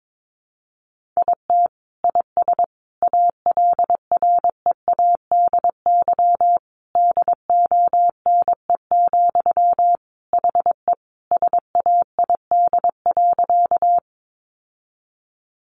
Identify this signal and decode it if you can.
Morse code, heard as a keyed tone, about 22 wpm, 700 Hz. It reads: IT IS ALREADY DONE, HE SAID.
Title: Sentences from Top 700 Words 22wpm